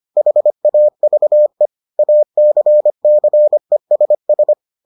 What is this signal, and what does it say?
Morse code, heard as a keyed tone, about 25 wpm, 600 Hz. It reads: HAVE ACCESS